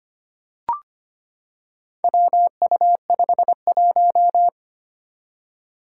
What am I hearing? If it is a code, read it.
Morse code, heard as a keyed tone, about 25 wpm, 700 Hz. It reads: WU51
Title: Four Challenging Letter-Number Combinations 25wpm